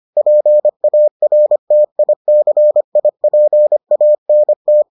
Morse code, heard as a keyed tone, about 25 wpm, 600 Hz. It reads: PARTICIPANT